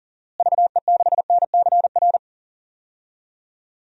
Morse code, heard as a keyed tone, about 40 wpm, 700 Hz. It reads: VE6NCR
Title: DXCC-100 Call Signs - 6 Characters - Repeated 3x - 40wpm